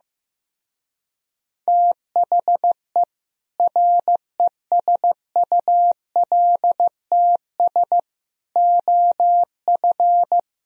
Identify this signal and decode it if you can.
Morse code, heard as a keyed tone, about 15 wpm, 700 Hz. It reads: THE RESULTS OF
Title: Speed-Racing - Sets of 5 Words 15wpm